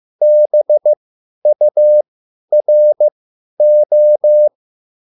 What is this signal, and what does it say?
Morse code, heard as a keyed tone, about 15 wpm, spaced as for 10 wpm, 600 Hz. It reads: BURO